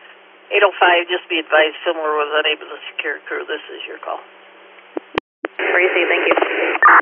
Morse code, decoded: none

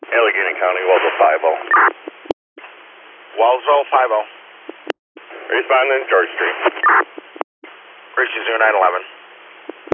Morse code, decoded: none

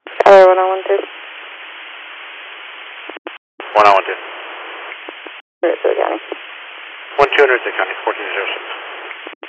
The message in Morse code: none

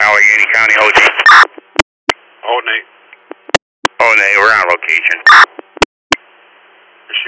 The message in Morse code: none

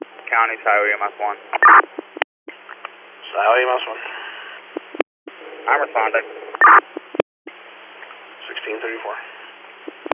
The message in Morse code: none

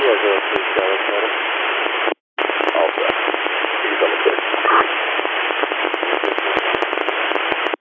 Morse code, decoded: none

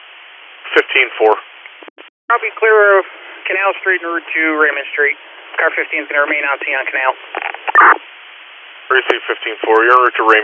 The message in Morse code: none

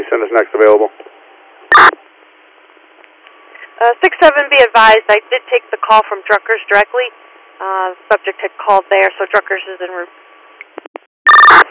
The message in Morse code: none

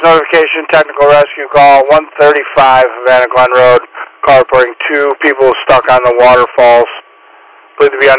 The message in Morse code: none

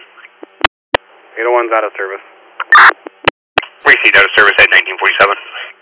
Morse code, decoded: none

